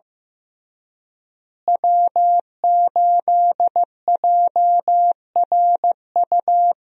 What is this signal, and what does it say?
Morse code, heard as a keyed tone, about 14 wpm, 700 Hz. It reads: W8JRU
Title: USA Call Signs - 5 Characters - Repeated 3x - 15wpm